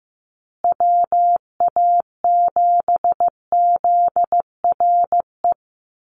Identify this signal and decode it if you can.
Morse code, heard as a keyed tone, about 15 wpm, 700 Hz. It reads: WA7ZRE
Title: USA Call Signs Repeated 2x 15wpm